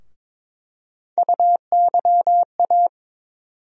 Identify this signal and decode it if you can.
Morse code, heard as a keyed tone, about 22 wpm, 700 Hz. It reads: UYA
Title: USA Call Sign Postfixes - 3 Letters 22wpm